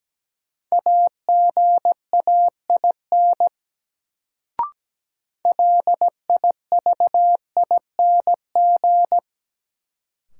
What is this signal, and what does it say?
Morse code, heard as a keyed tone, about 17 wpm, 700 Hz. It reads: AGAIN LIVING
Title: Sets of 4 Words - New Word Drill - 17wpm